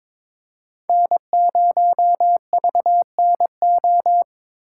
Morse code, heard as a keyed tone, about 26 wpm, 700 Hz. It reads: N0VNO
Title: Speed Racing - USA 1x3 Call Signs 22wpm